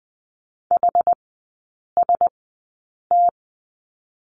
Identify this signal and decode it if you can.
Morse code, heard as a keyed tone, about 20 wpm, spaced as for 8 wpm, 700 Hz. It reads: HST
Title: 067 Lesson 12e - Letter d - Three Characters 20wpm